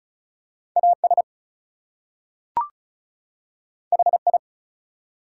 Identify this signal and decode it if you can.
Morse code, heard as a keyed tone, about 36 wpm, 700 Hz. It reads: AS HI
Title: Two Letter Words - 35wpm